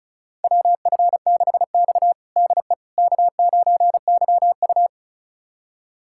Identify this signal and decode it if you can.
Morse code, heard as a keyed tone, about 36 wpm, 700 Hz. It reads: WF6X DE K9YU
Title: USA Callsign DE Callsign - 4 Characters - 35wpm